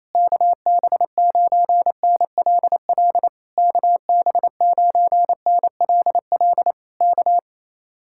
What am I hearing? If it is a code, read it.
Morse code, heard as a keyed tone, about 26 wpm, 700 Hz. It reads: KB9NLL KB9NLL K
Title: Speed-Racing Common QSO Phrases Encore - 28wpm